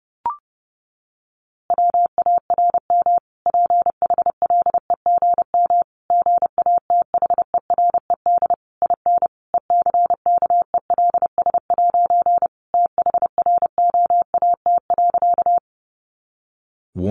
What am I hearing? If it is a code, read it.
Morse code, heard as a keyed tone, about 30 wpm, 700 Hz. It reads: WARM PHLEGM GATHERED IN ECKELS' THROAT.
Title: A Sound Of Thunder 30wpm